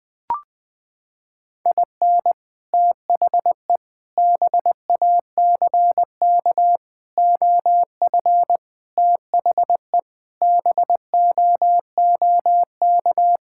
Morse code, heard as a keyed tone, about 20 wpm, 700 Hz. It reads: IN THE BACK OF THE BOOK
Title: Sentences with 5-Word Phrases - 20wpm